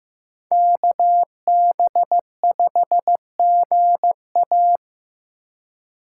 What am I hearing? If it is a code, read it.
Morse code, heard as a keyed tone, about 15 wpm, 700 Hz. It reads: KB5GA
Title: K1USN SST QSO Practice - 15wpm